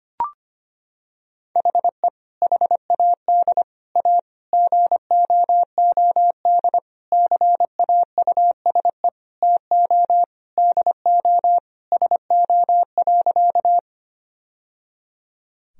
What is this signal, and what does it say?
Morse code, heard as a keyed tone, about 25 wpm, 700 Hz. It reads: HE HAD A GOOD CAUSE TO DO SO.
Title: Sentences from Top 1000 Words - Review Part 1 - 25wpm